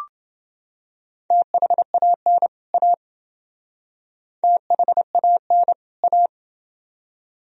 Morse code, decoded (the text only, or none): THAN A THAN A